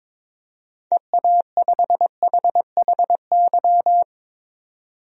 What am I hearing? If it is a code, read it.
Morse code, heard as a keyed tone, about 22 wpm, 700 Hz. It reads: EA5HHY
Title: DXCC-100 Call Signs - 6 Characters - 22wpm